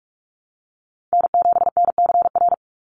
Morse code, heard as a keyed tone, about 44 wpm, 700 Hz. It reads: N7NCR
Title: USA Call Signs 45wpm